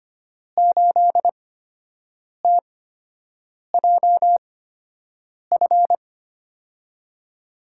Morse code, heard as a keyed tone, about 25 wpm, spaced as for 7 wpm, 700 Hz. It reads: 8TJF
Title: USA Call Signs - 6 Characters - 8x Character Spacing - 25wpm